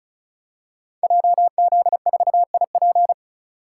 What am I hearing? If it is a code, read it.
Morse code, heard as a keyed tone, about 36 wpm, 700 Hz. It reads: JZ4IP
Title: Code Groups - 5 Letter:Numbers - 35wpm